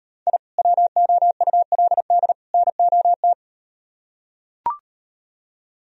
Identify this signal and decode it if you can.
Morse code, heard as a keyed tone, about 38 wpm, 700 Hz. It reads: I WOULD NOT